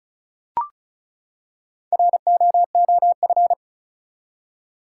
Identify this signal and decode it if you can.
Morse code, heard as a keyed tone, about 36 wpm, 700 Hz. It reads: ROOF
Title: Four Letter Words - 35wpm